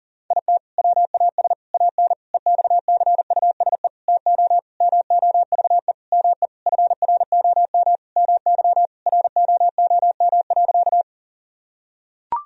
Morse code, heard as a keyed tone, about 40 wpm, 700 Hz. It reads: IT WAS AN EXCUSE TO MOVE ME FROM MY ROOM.
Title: Sentences from Top 1000 Words - Review Part 1 - 40wpm